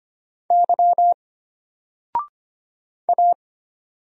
Morse code, heard as a keyed tone, about 25 wpm, 700 Hz. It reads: Y A
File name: Single-Letter-Number-25wpm